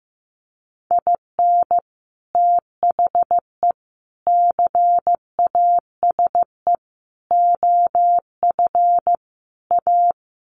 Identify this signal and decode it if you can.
Morse code, heard as a keyed tone, about 15 wpm, 700 Hz. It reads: IN THE CASE OF A
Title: Sets of 5 Words Encore 15wpm